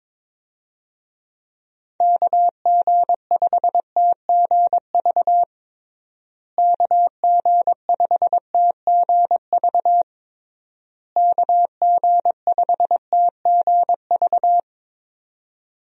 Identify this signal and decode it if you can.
Morse code, heard as a keyed tone, about 22 wpm, 700 Hz. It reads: KG5TGV KG5TGV KG5TGV
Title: USA Call Signs - 6 Characters - Repeated 3x - 22wpm